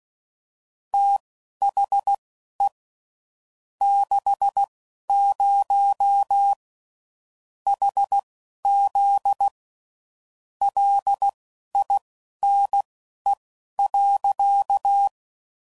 Morse code, heard as a keyed tone, about 16 wpm, spaced as for 11 wpm, 800 Hz. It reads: THE 60 HZ LINE.
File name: CW Code Groups @10WPM / Technical Subjects / SwPwrSup